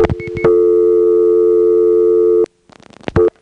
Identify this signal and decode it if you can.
Dial tone (North America), 350 + 440 Hz.
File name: GroupBell / MariettaMap-unfinished / Marietta1